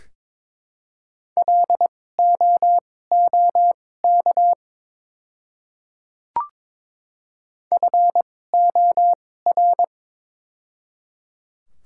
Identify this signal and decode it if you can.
Morse code, heard as a keyed tone, about 22 wpm, spaced as for 16 wpm, 690 Hz. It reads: LOOK FOR
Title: 2x Character Spacing - Top 100 Words - 22wpm